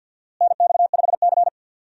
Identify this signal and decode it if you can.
Morse code, heard as a keyed tone, about 50 wpm, 700 Hz. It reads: NX5C
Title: USA Call Signs - 4 Characters - Repeated 2x - 50wpm